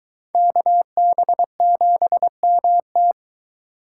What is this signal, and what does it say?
Morse code, heard as a keyed tone, about 22 wpm, 700 Hz. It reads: KB7MT